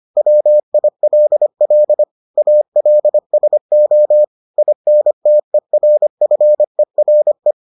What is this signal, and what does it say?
Morse code, heard as a keyed tone, about 25 wpm, 600 Hz. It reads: WILL ALSO INTERFERE